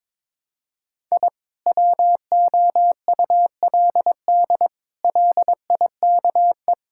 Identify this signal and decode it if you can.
Morse code, heard as a keyed tone, about 22 wpm, 700 Hz. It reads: I WOULD LIKE
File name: practice / 22wpm / Sets of 8 Words - 22wpm